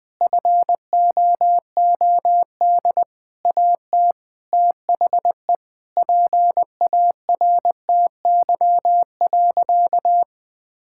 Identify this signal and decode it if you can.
Morse code, heard as a keyed tone, about 20 wpm, 700 Hz. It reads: FOOD AT THE PARTY.
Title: Sentences with 5-Word Phrases - 20wpm